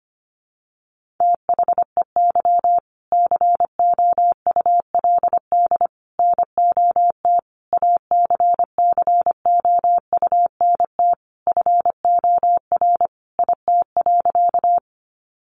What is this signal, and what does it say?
Morse code, heard as a keyed tone, about 25 wpm, 700 Hz. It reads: THEY COULD NOT ACCOUNT FOR IT.
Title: Sentences from Top 900 Words 25wpm